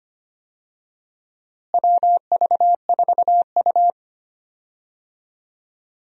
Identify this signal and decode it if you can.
Morse code, heard as a keyed tone, about 25 wpm, 700 Hz. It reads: WV4U
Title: USA Call Signs - 4 Characters - Repeated 3x - 25wpm